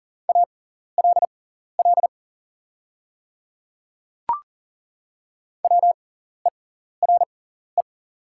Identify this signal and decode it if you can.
Morse code, heard as a keyed tone, about 40 wpm, spaced as for 14 wpm, 700 Hz. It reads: ALL WERE
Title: 6x Character Spacing - Top 100 Words - 40wpm